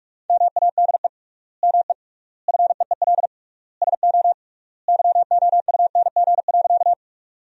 Morse code, heard as a keyed tone, about 46 wpm, 700 Hz. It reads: MADE ME FEEL SO YOUNG.
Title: 3x Word-Spacing - Sentences from Top 400 Words - 45wpm